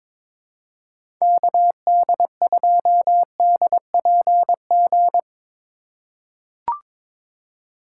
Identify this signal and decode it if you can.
Morse code, heard as a keyed tone, about 23 wpm, 700 Hz. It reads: KD2DPG
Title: USA Call Signs - 6 Characters - 3x Character Spacing - 22wpm